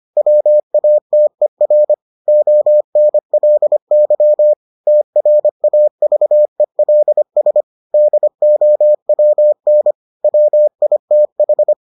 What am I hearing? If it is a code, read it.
Morse code, heard as a keyed tone, about 25 wpm, 600 Hz. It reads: WATER ONLY TRAVELS DOWN WITH